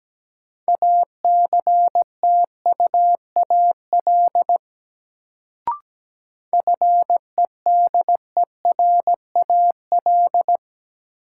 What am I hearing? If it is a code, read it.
Morse code, heard as a keyed tone, about 17 wpm, 700 Hz. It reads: ACTUAL FEDERAL
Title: Sentences from Top 1000 Words - Review Part 5 - New Word Drill - 17wpm